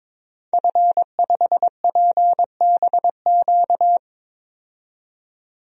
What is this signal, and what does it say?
Morse code, heard as a keyed tone, about 23 wpm, 700 Hz. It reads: F5PBQ
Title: DXCC-100 Call Signs - 5 Characters - 22wpm